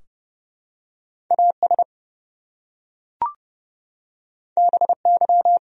Morse code, heard as a keyed tone, about 29 wpm, 700 Hz. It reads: AS BY
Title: Top 100 Words Repeated 30wpm